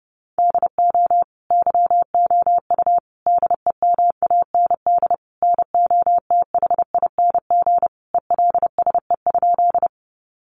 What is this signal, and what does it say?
Morse code, heard as a keyed tone, about 30 wpm, 700 Hz. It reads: DO YOU DEMAND NOTHING ELSE?
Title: Sentences from Top 800 Words 30wpm